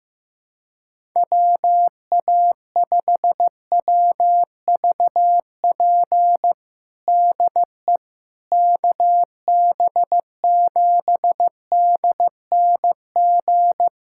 Morse code, heard as a keyed tone, about 15 wpm, 700 Hz. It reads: WA5WVP DE KB7DNG
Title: USA Callsign DE Callsign - 6 Characters - 15wpm